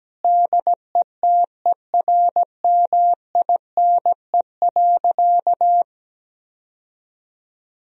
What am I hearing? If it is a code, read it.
Morse code, heard as a keyed tone, about 17 wpm, 700 Hz. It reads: DETERMINE.
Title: Sentences from Top 1000 Words - Review Part 4 - 17wpm